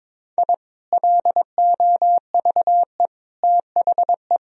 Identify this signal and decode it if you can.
Morse code, heard as a keyed tone, about 22 wpm, 700 Hz. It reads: I LOVE THE
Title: Sentences from Top 1000 Words - Review Part 3 - 22wpm